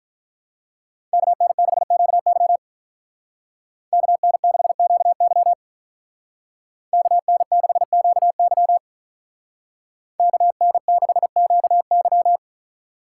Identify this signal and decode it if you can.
Morse code, heard as a keyed tone, about 43 wpm, 700 Hz. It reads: KN6QY KN6QY KN6QY KN6QY
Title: Speed Racing - USA 2x2 Call Signs 35wpm